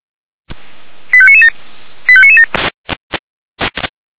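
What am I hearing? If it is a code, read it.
Five-tone selective call (ZVEI), heard as quick runs of five tones, tones of about 70 ms. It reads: 85997 85997
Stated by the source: ZVEI five-tone selective call 85997 received on an SDR.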